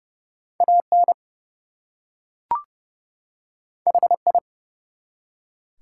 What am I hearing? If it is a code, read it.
Morse code, heard as a keyed tone, about 30 wpm, 700 Hz. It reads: AN HI
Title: Two Letter Words - 30wpm